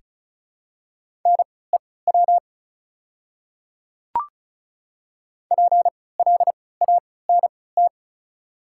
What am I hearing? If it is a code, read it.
Morse code, heard as a keyed tone, about 36 wpm, spaced as for 20 wpm, 690 Hz. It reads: NEW PLANT
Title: 3x Character Spacing - Top 300 Words - 35wpm